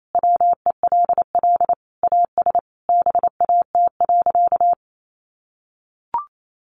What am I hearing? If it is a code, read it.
Morse code, heard as a keyed tone, about 28 wpm, 700 Hz. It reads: WELL AS BAT.
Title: Sentences from Top 1500 Words 28wpm